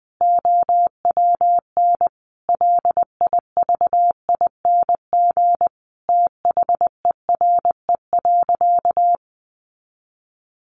Morse code, heard as a keyed tone, about 20 wpm, 700 Hz. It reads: OWN LIVING THERE.
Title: Sentences from Top 1500 Words 20wpm